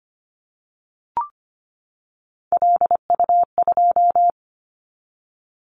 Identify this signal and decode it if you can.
Morse code, heard as a keyed tone, about 25 wpm, 700 Hz. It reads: LU2
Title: Callsign Prefixes for Top 100 DXCC - 25wpm